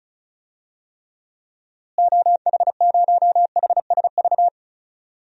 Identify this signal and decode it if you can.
Morse code, heard as a keyed tone, about 36 wpm, 700 Hz. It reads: OH0HSV
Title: DXCC-100 Call Signs - 6 Characters - Repeated 2x - 35wpm